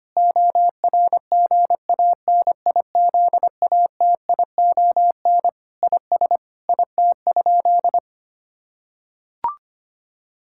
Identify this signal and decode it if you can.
Morse code, heard as a keyed tone, about 25 wpm, 700 Hz. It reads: ORGANIZATION IS IT?
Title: Sentences from Top 1000 Words - Review Part 5 - 25wpm